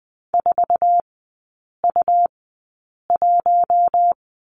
Morse code, heard as a keyed tone, about 20 wpm, spaced as for 8 wpm, 700 Hz. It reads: 4U1
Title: 166 Lesson 27e - Number 7 - Three Characters 20wpm